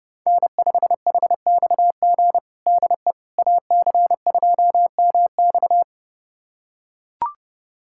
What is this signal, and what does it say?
Morse code, heard as a keyed tone, about 32 wpm, 700 Hz. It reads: N5HXG DE AC2MX
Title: USA Callsign DE Callsign - 5 Characters - 30wpm